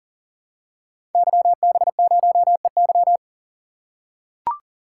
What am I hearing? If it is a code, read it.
Morse code, heard as a keyed tone, about 40 wpm, 700 Hz. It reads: YB0EY
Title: Code Groups - 5 Letter:Numbers - 40wpm